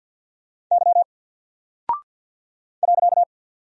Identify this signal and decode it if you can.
Morse code, heard as a keyed tone, about 49 wpm, 700 Hz. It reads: Y .
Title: Single Character (Letter:Number:Punctuation:Prosign) 50wpm